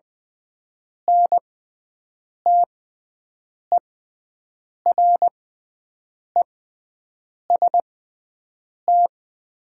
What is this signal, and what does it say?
Morse code, heard as a keyed tone, about 20 wpm, spaced as for 7 wpm, 700 Hz. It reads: NTEREST
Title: 6x Character Spacing - Top 500 Words - 20wpm